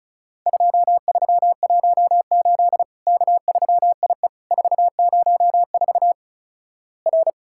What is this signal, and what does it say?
Morse code, heard as a keyed tone, about 36 wpm, 700 Hz. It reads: 2318 K3IE 404 R
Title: Worked All Europe DX Contest - QTC Exchange - 35wpm